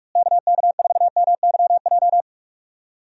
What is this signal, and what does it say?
Morse code, heard as a keyed tone, about 47 wpm, 700 Hz. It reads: KK4MYJ
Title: USA Call Signs - 6 Characters - Repeated 2x - 45wpm